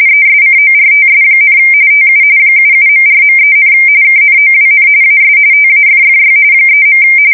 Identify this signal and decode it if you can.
RTTY, about 45.45 baud. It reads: ESCORTS, TU 77.4.3, KNOWN AS TAFFY 3.
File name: fsk170 / USS-Massachusetts